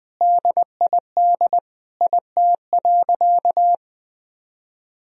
Morse code, heard as a keyed tone, about 20 wpm, 700 Hz. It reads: DID IT.
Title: Sentences from Top 1000 Words - Review Part 3 - 20wpm